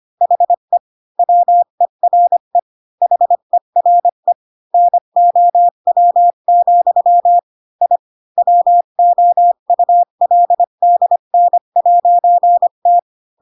Morse code, heard as a keyed tone, about 25 wpm, 700 Hz. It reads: HE WERE HERE NOW, I WOULDN'T
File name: Pinocchio / Pinocchio ch5 20 25 ik1hge